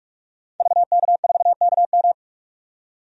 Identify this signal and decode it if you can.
Morse code, heard as a keyed tone, about 42 wpm, 700 Hz. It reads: VK4KM
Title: DXCC-100 Call Signs - 5 Characters - 45wpm